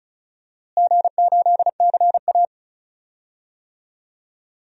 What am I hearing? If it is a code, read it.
Morse code, heard as a keyed tone, about 36 wpm, 700 Hz. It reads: G8CA
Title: DXCC-100 Call Signs - 4 Characters - Repeated 2x - 35wpm